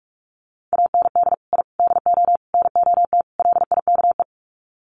Morse code, heard as a keyed tone, about 44 wpm, 700 Hz. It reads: AND I DO NOT LIKE
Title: Sets of 5 Words Encore2 45wpm